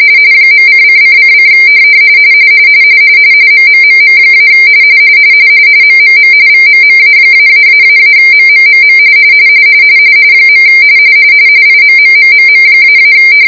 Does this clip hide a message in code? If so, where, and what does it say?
Yes: RTTY from 0.5 to 13.1 s, about 45.45 baud. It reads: .::::: MMMM- MMM VMMA-----MMMM . IMMMMM--